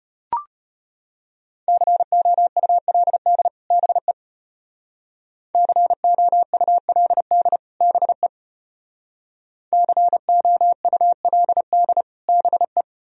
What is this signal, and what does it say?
Morse code, heard as a keyed tone, about 32 wpm, 700 Hz. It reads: COULD BE COULD BE COULD BE